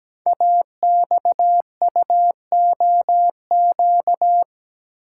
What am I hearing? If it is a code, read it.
Morse code, heard as a keyed tone, about 17 wpm, 700 Hz. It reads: AXUOQ
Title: Code Groups - 5 Letters - 17wpm